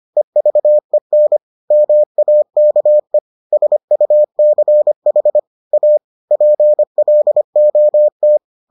Morse code, heard as a keyed tone, about 25 wpm, 600 Hz. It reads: EVEN MAKE SUCH A PLOT